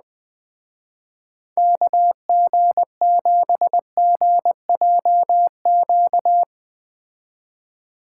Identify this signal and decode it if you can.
Morse code, heard as a keyed tone, about 19 wpm, 700 Hz. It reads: KG7GJQ
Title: Speed Racing - USA 2x3 Call Signs 17wpm